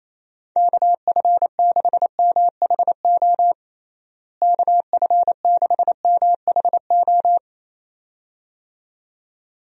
Morse code, heard as a keyed tone, about 27 wpm, 700 Hz. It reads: KF6MHO KF6MHO
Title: USA Call Signs - 6 Characters - Repeated 2x - 28wpm